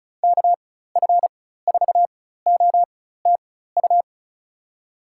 Morse code, heard as a keyed tone, about 36 wpm, spaced as for 16 wpm, 700 Hz. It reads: KF4OTU
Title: USA Call Signs - 6 Characters - 4x Character Spacing - 35wpm